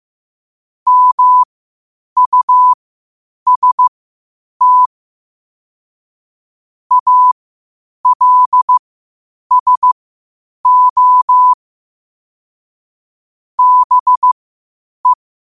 Morse code, heard as a keyed tone, about 15 wpm, spaced as for 8 wpm, 1000 Hz. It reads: MUST ALSO BE
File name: qotdfiles / 8wpm